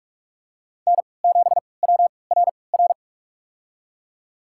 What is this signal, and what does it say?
Morse code, heard as a keyed tone, about 43 wpm, spaced as for 25 wpm, 700 Hz. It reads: N7WRR
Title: USA Call Signs - 5 Characters - 3x Character Spacing - 45wpm